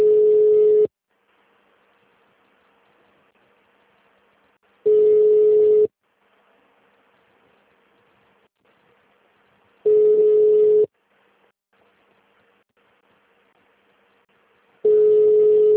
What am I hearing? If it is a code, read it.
Ringback tone (Europe, 425 Hz).
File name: out- -2002- - - .42